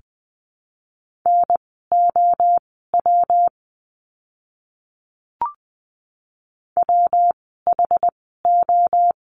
Morse code, heard as a keyed tone, about 20 wpm, spaced as for 15 wpm, 700 Hz. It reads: NOW WHO
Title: 139 Lesson 23f - Letter f - Words 20wpm